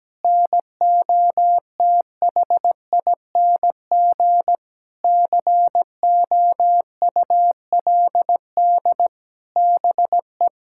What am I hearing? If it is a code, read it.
Morse code, heard as a keyed tone, about 17 wpm, 700 Hz. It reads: NOTHING COULD BE
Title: Sentences from Top 1000 Words - Review Part 6 - 17wpm